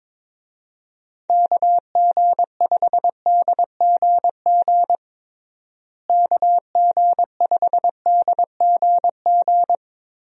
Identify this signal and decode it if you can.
Morse code, heard as a keyed tone, about 22 wpm, 700 Hz. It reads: KG5DGG KG5DGG
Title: USA Call Signs - 6 Characters - Repeated 3x - 22wpm